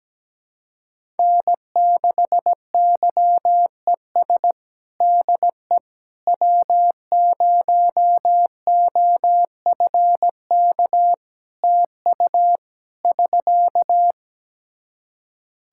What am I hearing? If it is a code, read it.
Morse code, heard as a keyed tone, about 17 wpm, 700 Hz. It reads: N6YES DE W0OFK TU <SK>
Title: K1USN SST QSO Practice 17wpm